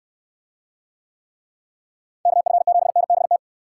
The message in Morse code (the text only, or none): KF7NBT